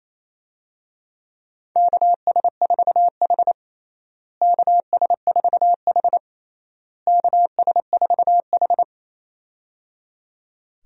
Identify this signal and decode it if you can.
Morse code, heard as a keyed tone, about 28 wpm, 700 Hz. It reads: KS4H KS4H KS4H
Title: USA Call Signs - 4 Characters - Repeated 3x - 28wpm